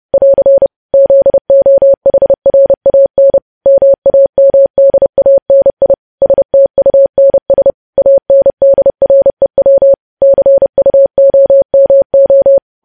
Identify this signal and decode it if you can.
Morse code, heard as a keyed tone, about 30 wpm, 550 Hz. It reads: <AR> ZOHRAN MAMDANI STUNS ANDREW CUOMO
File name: podcast / News headlines-1x-30x30